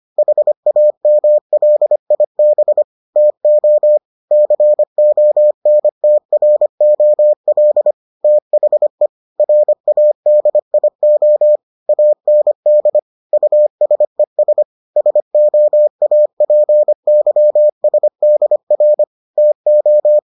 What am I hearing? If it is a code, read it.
Morse code, heard as a keyed tone, about 25 wpm, 600 Hz. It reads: HAMLIB TO CONTROL THE RADIO AND USES SOAPYSDR TO